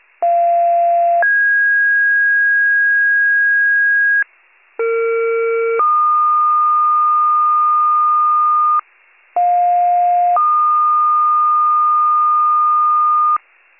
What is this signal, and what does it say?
Two-tone page, heard as a one-second tone, then a three-second tone, about 690 then 1650 Hz, about 460 then 1220 Hz, about 710 then 1220 Hz. It is heard from 0.2 to 13.4 s.